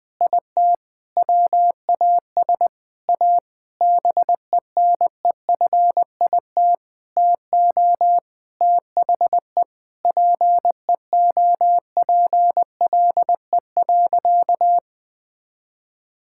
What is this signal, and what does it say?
Morse code, heard as a keyed tone, about 20 wpm, 700 Hz. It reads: IT WAS A BENEFIT TO THE PEOPLE.
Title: Sentences from Top 1000 Words - Review Part 6 - 20wpm